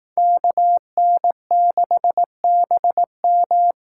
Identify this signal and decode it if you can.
Morse code, heard as a keyed tone, about 18 wpm, 700 Hz. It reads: KN6BM
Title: Speed Racing - USA 2x2 Call Signs 15wpm